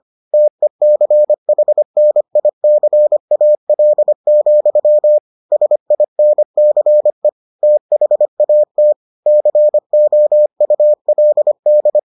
Morse code, heard as a keyed tone, about 25 wpm, 600 Hz. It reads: TECHNICAL, SINCE THAT COULD